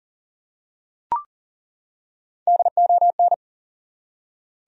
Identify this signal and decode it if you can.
Morse code, heard as a keyed tone, about 40 wpm, 700 Hz. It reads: DON
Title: USA Common Names - Max 5 Letters - 40wpm